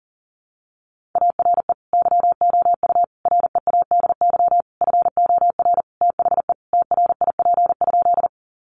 Morse code, heard as a keyed tone, about 40 wpm, 700 Hz. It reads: ARE YOU READY FOR THE TRIP?
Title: Sentences from Top 1300 Words 40wpm